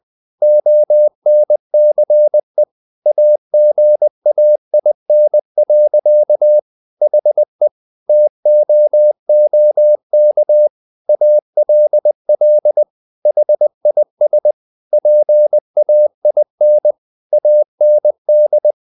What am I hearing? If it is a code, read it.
Morse code, heard as a keyed tone, about 20 wpm, 600 Hz. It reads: ONCE AGAIN. HE TOOK ALL HIS PAIN AND
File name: the old man and the sea part 2 20wpm